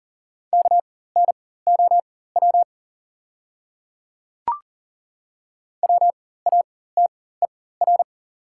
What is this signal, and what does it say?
Morse code, heard as a keyed tone, about 40 wpm, spaced as for 19 wpm, 690 Hz. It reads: KNOW WATER
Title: 4x Character Spacing - Top 100 Words - 40wpm